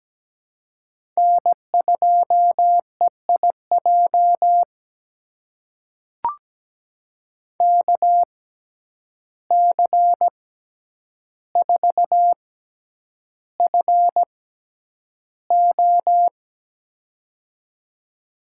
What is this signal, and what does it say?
Morse code, heard as a keyed tone, about 17 wpm, spaced as for 9 wpm, 700 Hz. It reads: N2EIJ KC4FO
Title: USA Call Signs - 5 Characters - 6x Character Spacing - 17wpm